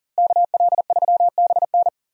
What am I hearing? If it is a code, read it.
Morse code, heard as a keyed tone, about 39 wpm, 700 Hz. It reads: KL3BN
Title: USA Call Signs - 5 Characters - Repeated 3x - 40wpm